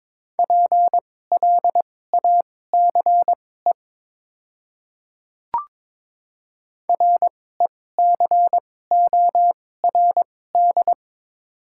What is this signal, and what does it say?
Morse code, heard as a keyed tone, about 22 wpm, spaced as for 16 wpm, 700 Hz. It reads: PLACE RECORD
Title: 2x Character Spacing - Top 500 Words - 22wpm